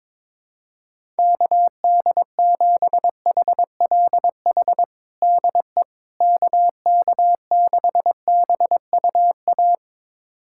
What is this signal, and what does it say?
Morse code, heard as a keyed tone, about 22 wpm, 700 Hz. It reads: KD7HLH DE KK6BUA
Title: USA Callsign DE Callsign - 6 Characters - 22wpm